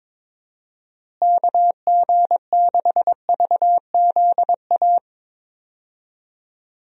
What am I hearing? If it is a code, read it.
Morse code, heard as a keyed tone, about 22 wpm, 700 Hz. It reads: KG6VZA
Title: USA Call Signs - 6 Characters - Repeated 3x - 22wpm